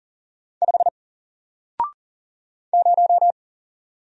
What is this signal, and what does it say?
Morse code, heard as a keyed tone, about 40 wpm, 700 Hz. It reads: 5 0
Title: Single Character (Letter:Number:Punctuation:Prosign) Alt 40wpm